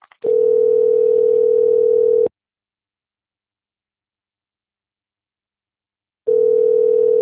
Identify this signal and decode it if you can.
Ringback tone (North America), 440 + 480 Hz.
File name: rg-1600-unknown- - - .4752